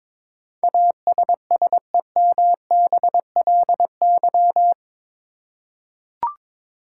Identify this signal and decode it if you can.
Morse code, heard as a keyed tone, about 22 wpm, 700 Hz. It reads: ASSEMBLY
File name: practice / 22wpm / Eight Letter Words - 22wpm